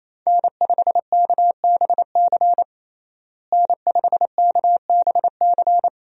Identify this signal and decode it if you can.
Morse code, heard as a keyed tone, about 28 wpm, 700 Hz. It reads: N5KBC N5KBC
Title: USA Call Signs - 5 Characters - Repeated 3x - 28wpm